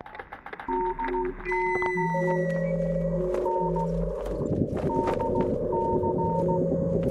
Morse code, heard as a keyed tone, about 16 wpm, 910 Hz, with noise. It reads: DC N DC